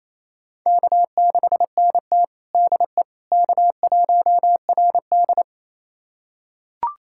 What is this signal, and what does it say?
Morse code, heard as a keyed tone, about 28 wpm, 700 Hz. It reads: K6NT DE K1RD
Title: USA Callsign DE Callsign - 4 Characters - 28wpm